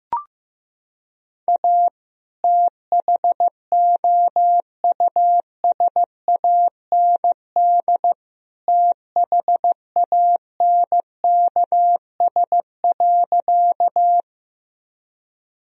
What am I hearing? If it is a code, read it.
Morse code, heard as a keyed tone, about 15 wpm, 700 Hz. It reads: A THOUSAND THANKS.
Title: Sentences from Top 1000 Words - Review Part 3 - 15wpm